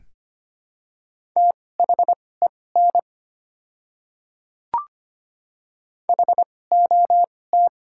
Morse code, heard as a keyed tone, about 25 wpm, spaced as for 18 wpm, 700 Hz. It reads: THEN HOT
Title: 2x Character Spacing - Top 300 Words - 25wpm